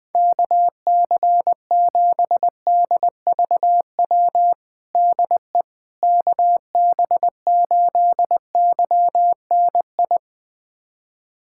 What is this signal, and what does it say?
Morse code, heard as a keyed tone, about 21 wpm, 700 Hz. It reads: KC7DVW DE KB8YNI
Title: USA Callsign DE Callsign 20wpm